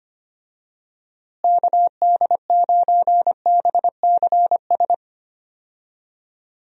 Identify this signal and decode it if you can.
Morse code, heard as a keyed tone, about 25 wpm, 700 Hz. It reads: KD9BCS
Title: USA Call Signs - 6 Characters - Repeated 3x - 25wpm